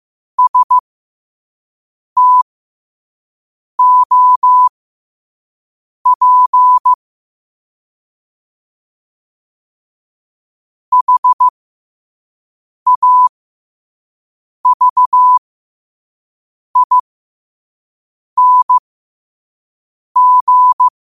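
Morse code, heard as a keyed tone, about 15 wpm, spaced as for 5 wpm, 1000 Hz. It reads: STOP HAVING
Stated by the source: Morse quote-of-the-day practice at 5 wpm.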